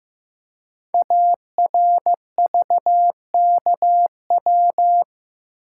Morse code, heard as a keyed tone, about 15 wpm, 700 Hz. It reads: ARVKW
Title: Code Groups - 5 Letters - 15wpm